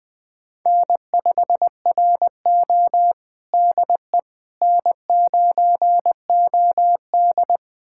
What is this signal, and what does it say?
Morse code, heard as a keyed tone, about 20 wpm, 700 Hz. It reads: N5RO DE N9OD
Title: USA Callsign DE Callsign - 4 Characters - 20wpm